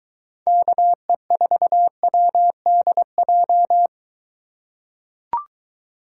Morse code, heard as a keyed tone, about 23 wpm, 700 Hz. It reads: KE4WDJ